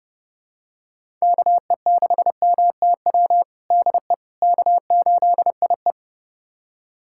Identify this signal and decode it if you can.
Morse code, heard as a keyed tone, about 30 wpm, 700 Hz. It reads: KE6MTW DE K8IE
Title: USA Callsign DE Callsign 30wpm